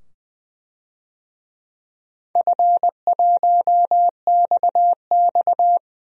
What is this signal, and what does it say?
Morse code, heard as a keyed tone, about 20 wpm, 700 Hz. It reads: F1XX
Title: DXCC-100 Call Signs - 4 Characters - 20wpm